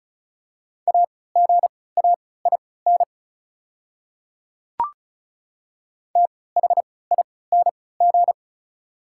Morse code, heard as a keyed tone, about 34 wpm, spaced as for 20 wpm, 700 Hz. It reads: AGAIN THING
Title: 3x Character Spacing - Top 200 Words - 35wpm